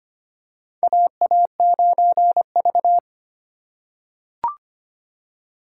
Morse code, heard as a keyed tone, about 25 wpm, 700 Hz. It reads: AA9V
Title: USA Call Signs - 4 Characters - 4x Character Spacing - 25wpm